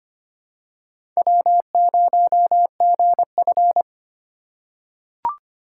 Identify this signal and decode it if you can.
Morse code, heard as a keyed tone, about 25 wpm, 700 Hz. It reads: W0GF
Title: USA Call Signs - 4 Characters - 3x Character Spacing - 25wpm